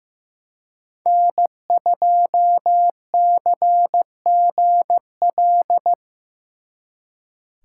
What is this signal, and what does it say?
Morse code, heard as a keyed tone, about 18 wpm, 700 Hz. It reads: N2CGL
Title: Speed Racing - USA 1x3 Call Signs 15wpm